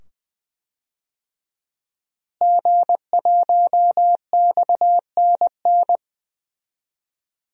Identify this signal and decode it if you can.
Morse code, heard as a keyed tone, about 20 wpm, 700 Hz. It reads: G1XNN
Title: DXCC-100 Call Signs - 5 Characters - Repeated 2x - 20wpm